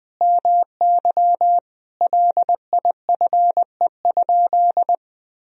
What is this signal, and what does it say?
Morse code, heard as a keyed tone, about 20 wpm, 700 Hz. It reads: MY LIFE?
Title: Sentences from Top 300 Words and Abbreviations 20wpm